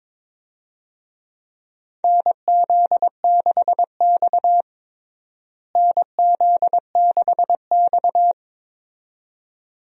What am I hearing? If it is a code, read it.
Morse code, heard as a keyed tone, about 22 wpm, 700 Hz. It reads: NZ6X NZ6X
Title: USA Call Signs - 4 Characters - Repeated 2x - 22wpm